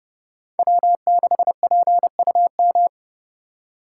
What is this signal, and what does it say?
Morse code, heard as a keyed tone, about 28 wpm, 700 Hz. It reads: W6PUM